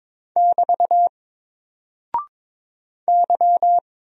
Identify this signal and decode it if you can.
Morse code, heard as a keyed tone, about 23 wpm, 700 Hz. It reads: <BT> Y
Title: Single Character (Letter:Number:Punctuation:Prosign) 22wpm